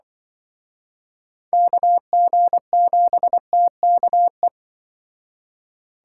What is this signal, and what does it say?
Morse code, heard as a keyed tone, about 25 wpm, 700 Hz. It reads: KG7TKE